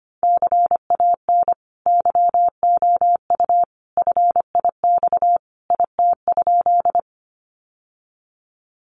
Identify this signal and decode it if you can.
Morse code, heard as a keyed tone, about 25 wpm, 700 Hz. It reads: CAN YOU FIX IT?
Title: Sentences from Top 1400 Words 25wpm